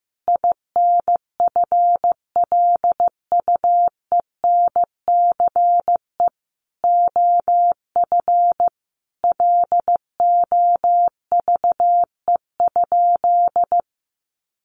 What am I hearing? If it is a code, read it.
Morse code, heard as a keyed tone, about 15 wpm, 700 Hz. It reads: INFLUENCE OF LOVE?
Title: Sentences from Top 1100 Words 15wpm